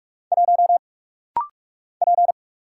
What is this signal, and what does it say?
Morse code, heard as a keyed tone, about 46 wpm, 700 Hz. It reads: 1 P
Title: Single Letter-Number Alt - Rapid-Fire 45wpm